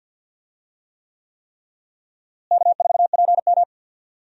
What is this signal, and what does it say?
Morse code, heard as a keyed tone, about 46 wpm, 700 Hz. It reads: K4PM
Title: USA Call Signs - 4 Characters - Repeated 3x - 50wpm